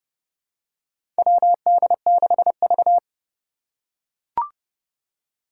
Morse code, heard as a keyed tone, about 30 wpm, 700 Hz. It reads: WD6V
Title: Four Challenging Letter-Number Combinations 30wpm